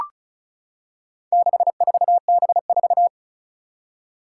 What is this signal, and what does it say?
Morse code, heard as a keyed tone, about 36 wpm, 700 Hz. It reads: 64B4
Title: Four Challenging Letter-Number Combinations 35wpm